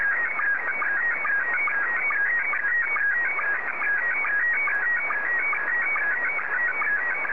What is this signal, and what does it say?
SSTV (slow-scan television), Scottie 1.